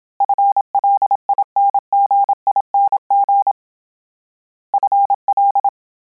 Morse code, heard as a keyed tone, about 27 wpm, 800 Hz. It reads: FLINGING FL